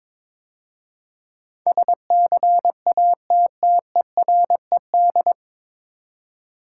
Morse code, heard as a keyed tone, about 22 wpm, 700 Hz. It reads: SCATTERED
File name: 9 Letter Words - First 3 Letters Given - 22wpm